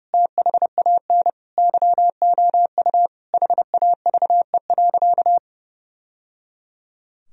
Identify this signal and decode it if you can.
Morse code, heard as a keyed tone, about 30 wpm, 700 Hz. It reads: THAN YOU HAVE.
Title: Sentences from Top 1000 Words - Review Part 3 - 30wpm